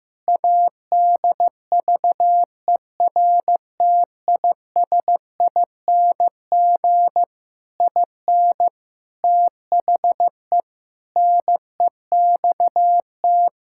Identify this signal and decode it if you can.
Morse code, heard as a keyed tone, about 15 wpm, 700 Hz. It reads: ADVERTISING IN THE NEXT
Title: Sentences from Top 1000 Words - Review Part 6 - 15wpm